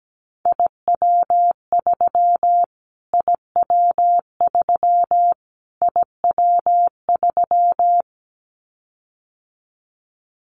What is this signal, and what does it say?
Morse code, heard as a keyed tone, about 17 wpm, 700 Hz. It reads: IW3 IW3 IW3
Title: Callsign Prefixes for Top 25 DXCC - Repeated 3x - 17wpm